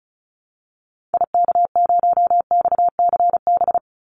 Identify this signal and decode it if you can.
Morse code, heard as a keyed tone, about 34 wpm, 700 Hz. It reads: IK0XCB
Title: Call Signs - 35wpm